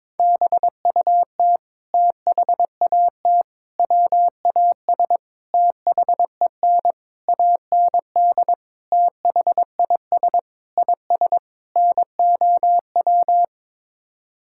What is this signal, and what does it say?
Morse code, heard as a keyed tone, about 22 wpm, 700 Hz. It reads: BUT THAT WAS THEN AND THIS IS NOW
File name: practice / 22wpm / Sets of 8 Words - 22wpm